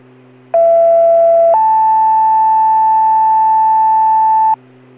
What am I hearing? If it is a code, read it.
Two-tone page, heard as a one-second tone, then a three-second tone, about 670 then 880 Hz, with noise. It is heard from 0.5 to 4.5 s.